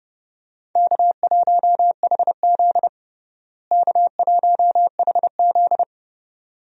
Morse code, heard as a keyed tone, about 32 wpm, 700 Hz. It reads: K1HZ K1HZ
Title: USA Call Signs - 4 Characters - Repeated 2x - 30wpm